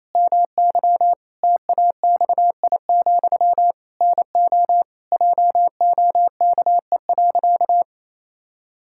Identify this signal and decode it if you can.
Morse code, heard as a keyed tone, about 28 wpm, 700 Hz. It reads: MY TAXI, NO JOKE.
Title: Pangram Challenge v2 - 28wpm